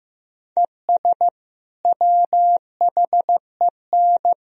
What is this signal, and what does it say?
Morse code, heard as a keyed tone, about 15 wpm, 700 Hz. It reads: ES WHEN
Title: Sentences from Top 100 Words and Abbreviations 15wpm